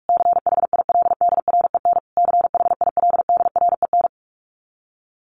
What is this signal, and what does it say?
Morse code, heard as a keyed tone, about 44 wpm, 700 Hz. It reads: CHILDREN CHILDREN
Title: Top 500 Words Repeated 45wpm